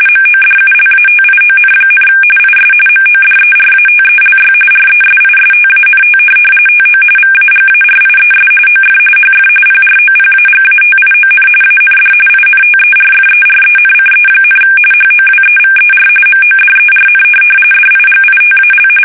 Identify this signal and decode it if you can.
RTTY, about 45.45 baud. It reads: MODULATION, IS MODERATELY RESISTANT TO VAGARIES OF HF PROPAGATION AND INTERFERENCE, HOWEVER MODERN DIGITAL
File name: fsk850 / teletype-RTTY